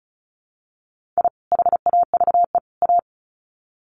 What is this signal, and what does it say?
Morse code, heard as a keyed tone, about 34 wpm, 700 Hz. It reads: I HAVE A
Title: Sets of 3 Words 35wpm